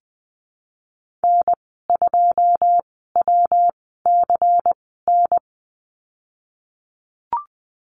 Morse code, heard as a keyed tone, about 20 wpm, spaced as for 15 wpm, 700 Hz. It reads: N2WCN
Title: 167 Lesson 27g - Number 7 - Call Signs 20wpm